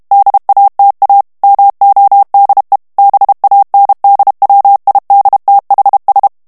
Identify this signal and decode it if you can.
Morse code, heard as a keyed tone, about 32 wpm, 800 Hz. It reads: DATA MODE BANDWIDTHS
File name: morse mp3 / ANGLAIS texte clair 3 vitesse 30 mots minute